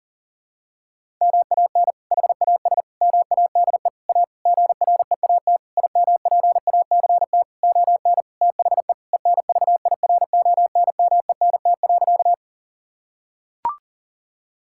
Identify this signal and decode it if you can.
Morse code, heard as a keyed tone, about 40 wpm, 700 Hz. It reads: MAN HAS MADE A GREAT IMPACT ON THE ENVIRONMENT.
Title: Sentences from Top 1000 Words - Review Part 6 - 40wpm